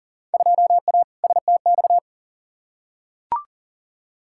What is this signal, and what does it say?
Morse code, heard as a keyed tone, about 40 wpm, 700 Hz. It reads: 2A STX
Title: Field Day Exchanges - 40wpm